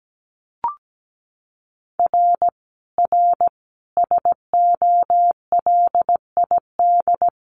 Morse code, heard as a keyed tone, about 17 wpm, 700 Hz. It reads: R R SOLID
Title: Common QSO Phrases 17wpm